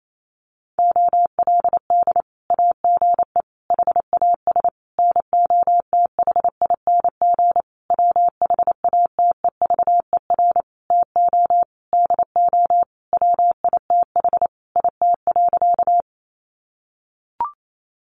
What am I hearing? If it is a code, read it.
Morse code, heard as a keyed tone, about 28 wpm, 700 Hz. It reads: OLD AGE HAS NOTHING WHATEVER TO DO WITH IT.
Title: Sentences from Top 1100 Words 28wpm